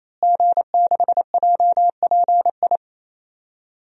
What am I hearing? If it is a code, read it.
Morse code, heard as a keyed tone, about 28 wpm, 700 Hz. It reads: G6JPI
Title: DXCC-100 Call Signs - 5 Characters - Repeated 3x - 28wpm